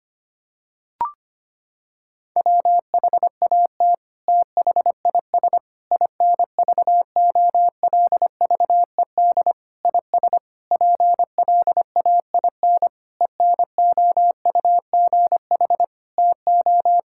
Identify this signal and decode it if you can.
Morse code, heard as a keyed tone, about 25 wpm, 700 Hz. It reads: WHAT THIS INVOLVED IS PLAIN ENOUGH TO
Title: Sentences from Top 1000 Words - Review Part 5 - 25wpm